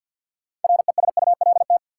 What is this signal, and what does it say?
Morse code, heard as a keyed tone, about 53 wpm, 700 Hz. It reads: RESULT
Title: Sets of 3 Words - New Word Drill - 50wpm